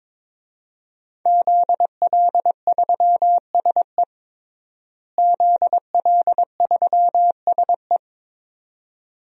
Morse code, heard as a keyed tone, about 21 wpm, 700 Hz. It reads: ZL3SE ZL3SE
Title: DXCC-100 Call Signs - 5 Characters - Repeated 2x - 22wpm